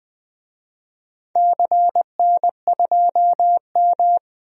Morse code, heard as a keyed tone, about 20 wpm, 700 Hz. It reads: CN2M
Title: DXCC-100 Call Signs - 4 Characters - 20wpm